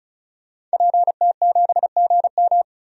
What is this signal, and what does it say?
Morse code, heard as a keyed tone, about 36 wpm, 700 Hz. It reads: PT7GM
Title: DXCC-100 Call Signs - 5 Characters - 35wpm